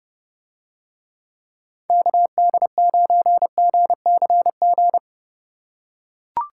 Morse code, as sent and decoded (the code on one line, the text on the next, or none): -.- -.. ----. --. -.-. --.
KD9GCG